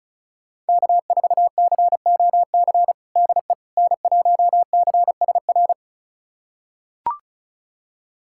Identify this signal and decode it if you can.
Morse code, heard as a keyed tone, about 36 wpm, 700 Hz. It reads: K4COC DE N1CSR
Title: USA Callsign DE Callsign 35wpm